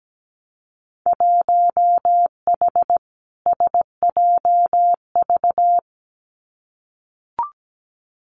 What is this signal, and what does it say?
Morse code, heard as a keyed tone, about 17 wpm, 700 Hz. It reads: 1H SJV
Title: Winter Field Day Exchanges 17wpm